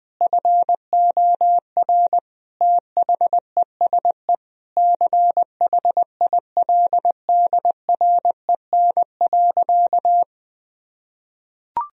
Morse code, heard as a keyed tone, about 20 wpm, 700 Hz. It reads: FOR THESE CHILDREN.
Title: Sentences from Top 1000 Words - Review Part 1 - 20wpm